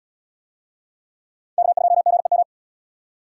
Morse code, heard as a keyed tone, about 78 wpm, 700 Hz. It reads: K2GEM